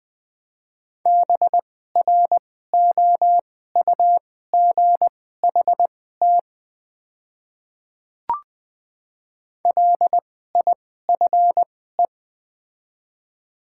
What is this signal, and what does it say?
Morse code, heard as a keyed tone, about 21 wpm, spaced as for 15 wpm, 690 Hz. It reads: BROUGHT LIFE
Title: 2x Character Spacing - Top 500 Words - 20wpm